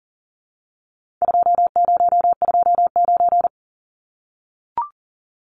Morse code, heard as a keyed tone, about 40 wpm, 700 Hz. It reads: 2029
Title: Numbers - 4-Digit 40wpm